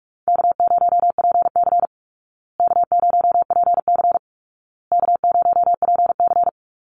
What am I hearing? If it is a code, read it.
Morse code, heard as a keyed tone, about 44 wpm, 700 Hz. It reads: K0PC K0PC K0PC
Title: Easy Call Signs Repeated 3x - 45wpm